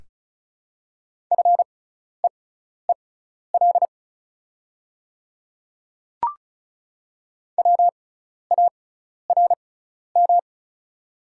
Morse code, heard as a keyed tone, about 36 wpm, spaced as for 12 wpm, 700 Hz. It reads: FEEL WARM
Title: 6x Character Spacing - Top 500 Words - 35wpm